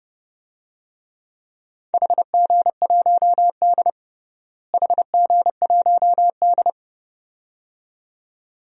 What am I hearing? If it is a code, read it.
Morse code, heard as a keyed tone, about 30 wpm, 700 Hz. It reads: HG1D HG1D
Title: DXCC-100 Call Signs - 4 Characters - Repeated 2x - 30wpm